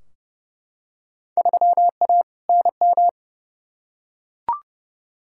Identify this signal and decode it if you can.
Morse code, heard as a keyed tone, about 32 wpm, 700 Hz. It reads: 3A NM
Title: Field Day Exchanges Encore - 30wpm